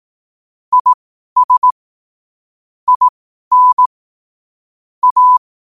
Morse code, heard as a keyed tone, about 18 wpm, spaced as for 12 wpm, 1000 Hz. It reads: IS IN A
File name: qotdfiles / 12wpm